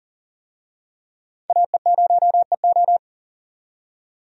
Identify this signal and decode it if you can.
Morse code, heard as a keyed tone, about 40 wpm, 700 Hz. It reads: AE0EO
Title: USA Call Signs - 5 Characters - 40wpm